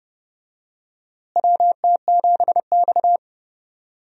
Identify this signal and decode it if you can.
Morse code, heard as a keyed tone, about 30 wpm, 700 Hz. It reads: WT7X
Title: USA Call Signs - 4 Characters - Repeated 3x - 30wpm